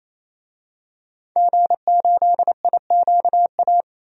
Morse code, heard as a keyed tone, about 28 wpm, 700 Hz. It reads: G8IQA
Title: DXCC-100 Call Signs - 5 Characters - 28wpm